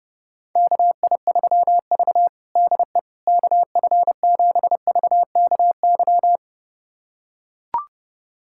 Morse code, heard as a keyed tone, about 32 wpm, 700 Hz. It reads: KI3V DE KF7VKY
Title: USA Callsign DE Callsign 30wpm